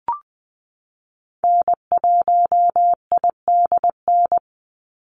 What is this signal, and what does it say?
Morse code, heard as a keyed tone, about 20 wpm, 700 Hz. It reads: N1IDN
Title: NA SOTA Activator Callsigns 20wpm